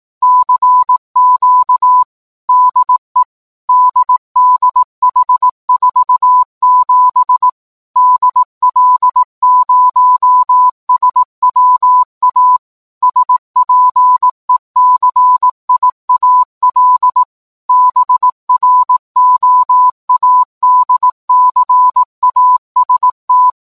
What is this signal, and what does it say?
Morse code, heard as a keyed tone, about 18 wpm, 1000 Hz. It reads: CQ DE DDH47 DL0SWA SPECIAL BROADCAST
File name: download-morsecode-titanic